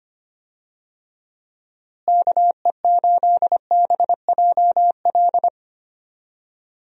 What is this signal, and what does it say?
Morse code, heard as a keyed tone, about 25 wpm, 700 Hz. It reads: KE8BJL
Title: USA Call Signs - 6 Characters - Repeated 2x - 25wpm